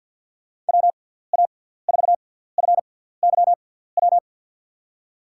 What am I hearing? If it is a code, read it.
Morse code, heard as a keyed tone, about 52 wpm, spaced as for 17 wpm, 700 Hz. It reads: WA4FYW
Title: USA Call Signs - 6 Characters - 6x Character Spacing - 50wpm